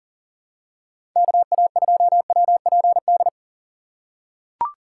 Morse code, heard as a keyed tone, about 40 wpm, 700 Hz. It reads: KA2WPD
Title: USA Call Signs - 6 Characters - 4x Character Spacing - 40wpm